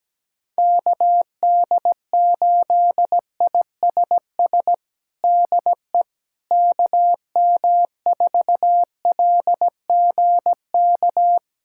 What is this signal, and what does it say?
Morse code, heard as a keyed tone, about 17 wpm, 700 Hz. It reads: KD8ISS DE KM4LGK
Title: USA Callsign DE Callsign - 6 Characters - 17wpm